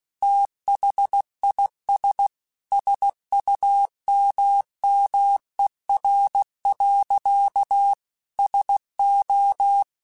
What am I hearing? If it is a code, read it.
Morse code, heard as a keyed tone, about 16 wpm, 800 Hz. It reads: THIS SUMMER. SO
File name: QSOs / QSOs-mp3s / WD0BGP 2@15